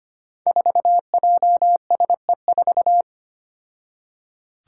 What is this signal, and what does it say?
Morse code, heard as a keyed tone, about 25 wpm, 700 Hz. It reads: 4JSE4
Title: Five Challenging Letter-Number Combinations 25wpm